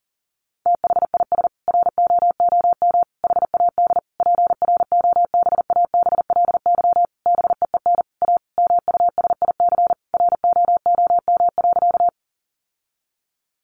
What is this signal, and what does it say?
Morse code, heard as a keyed tone, about 40 wpm, 700 Hz. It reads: THIS ROOM HAD PROBABLY BEEN A MUSIC ROOM.
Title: Sentences from Top 700 Words 40wpm